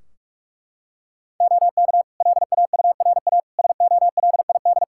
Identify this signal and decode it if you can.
Morse code, heard as a keyed tone, about 42 wpm, 700 Hz. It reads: OK LAURA SOLID
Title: K1USN SST QSO Practice 45wpm